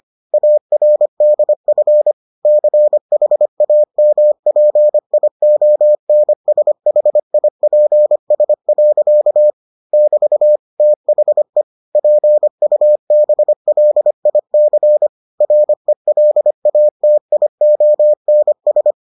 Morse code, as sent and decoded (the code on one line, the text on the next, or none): .- .-. -.. ..-. / -.-. .... .- -- .--. .. --- -. ... .... .. .--. ... .-.-.- / -...- / - .... / .--. ..- -... .-.. .. -.-. / .-. .-.. .- - .. --- -. ...
ARDF CHAMPIONSHIPS. <BT> THE PUBLIC RELATIONS